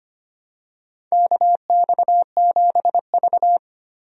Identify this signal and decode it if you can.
Morse code, heard as a keyed tone, about 25 wpm, 700 Hz. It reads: KX7V
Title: USA Call Signs - 4 Characters - Repeated 3x - 25wpm